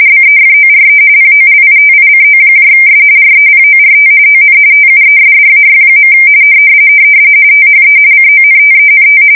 RTTY, about 45.45 baud. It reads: BEFORE POLITICOS REPORT ON TUESDAY. THE CONTENT
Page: 170 Shift 2125/2295